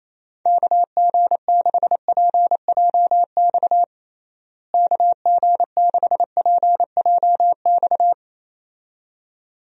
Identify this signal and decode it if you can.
Morse code, heard as a keyed tone, about 28 wpm, 700 Hz. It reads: KG6PJX KG6PJX
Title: USA Call Signs - 6 Characters - Repeated 2x - 28wpm